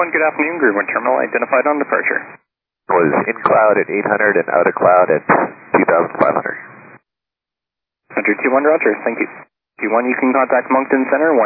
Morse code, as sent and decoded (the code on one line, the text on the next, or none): none